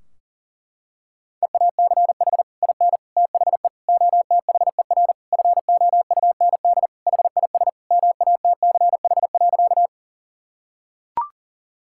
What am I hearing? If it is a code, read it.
Morse code, heard as a keyed tone, about 40 wpm, 700 Hz. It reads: EACH IN THE OTHER FOUND HIS MATCH.
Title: Sentences from Top 1600 Words 40wpm